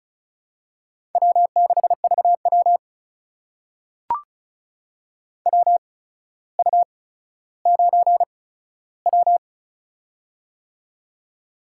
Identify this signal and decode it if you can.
Morse code, heard as a keyed tone, about 36 wpm, spaced as for 15 wpm, 700 Hz. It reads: W6VW WU9W
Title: USA Call Signs - 4 Characters - 8x Character Spacing - 35wpm